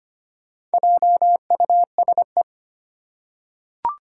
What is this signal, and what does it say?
Morse code, heard as a keyed tone, about 25 wpm, 700 Hz. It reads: JUSE
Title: Four Challenging Letter-Number Combinations 25wpm